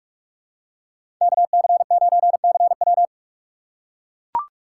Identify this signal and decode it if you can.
Morse code, heard as a keyed tone, about 43 wpm, 700 Hz. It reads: KC9CW
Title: USA Call Signs - 5 Characters - 2x Character Spacing - 45wpm